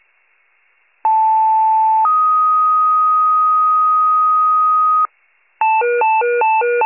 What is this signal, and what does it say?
Two-tone page, heard as a one-second tone, then a three-second tone, about 880 then 1250 Hz. It is heard from 1.0 to 5.1 s.
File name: pager / 76 FIRE / wcalert